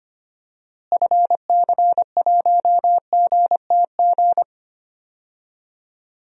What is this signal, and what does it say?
Morse code, heard as a keyed tone, about 25 wpm, 700 Hz. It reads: FC1GTG
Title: DXCC-100 Call Signs - 6 Characters - Repeated 3x - 25wpm